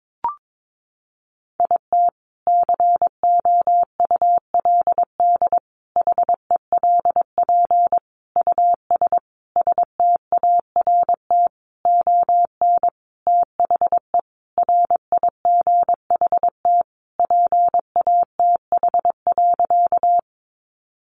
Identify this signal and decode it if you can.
Morse code, heard as a keyed tone, about 22 wpm, 700 Hz. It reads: IT COULD HELP US START ON THE RIGHT PATH.
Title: Sentences from Top 1400 Words 22wpm